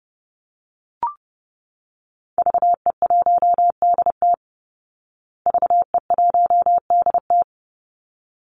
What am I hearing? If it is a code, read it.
Morse code, heard as a keyed tone, about 30 wpm, 700 Hz. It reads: VE1DT VE1DT
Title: Call Signs Repeated 3x - 30wpm